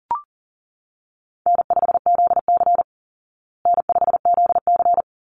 Morse code, heard as a keyed tone, about 40 wpm, 700 Hz. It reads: N5ZC N5ZC
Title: Call Signs Repeated 3x - 40wpm